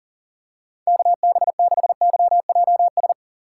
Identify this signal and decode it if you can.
Morse code, heard as a keyed tone, about 40 wpm, 700 Hz. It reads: KB6YJS
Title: USA Call Signs - 6 Characters - 4x Character Spacing - 40wpm